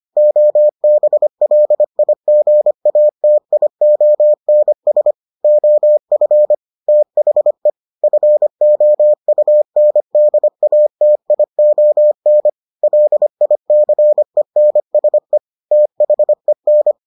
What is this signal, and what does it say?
Morse code, heard as a keyed tone, about 25 wpm, 600 Hz. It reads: OBLIGATIONS OF THE FOUNDATION LICENSE THEN